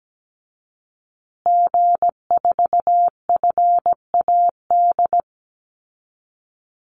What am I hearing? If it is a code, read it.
Morse code, heard as a keyed tone, about 17 wpm, 700 Hz. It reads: G4FAD
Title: Call Signs Repeated 2x - 17wpm